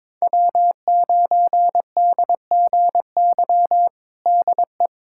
Morse code, heard as a keyed tone, about 22 wpm, 700 Hz. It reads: W9DGY DE
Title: USA Callsign DE Callsign - 5 Characters - 22wpm